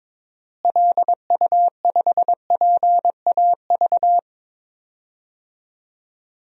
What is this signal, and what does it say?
Morse code, heard as a keyed tone, about 22 wpm, 700 Hz. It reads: LU5PAV
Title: DXCC-100 Call Signs - 6 Characters - Repeated 2x - 22wpm